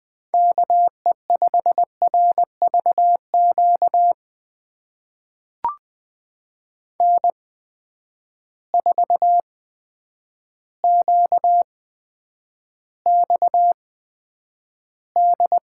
Morse code, heard as a keyed tone, about 20 wpm, spaced as for 9 wpm, 700 Hz. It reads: KE5RVQ N4QXD
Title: USA Call Signs - 8x Character Spacing - 20wpm